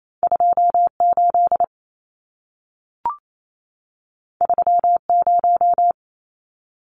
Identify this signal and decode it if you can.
Morse code, heard as a keyed tone, about 28 wpm, 700 Hz. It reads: 28 30
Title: Numbers - 2-Digit 28wpm